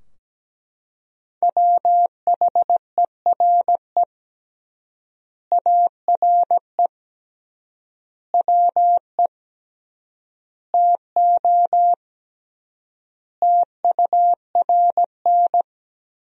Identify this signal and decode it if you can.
Morse code, heard as a keyed tone, about 17 wpm, 700 Hz. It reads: WHERE ARE WE TO TURN
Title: 3x Word-Spacing - Sentences from Top 200 Words - 17wpm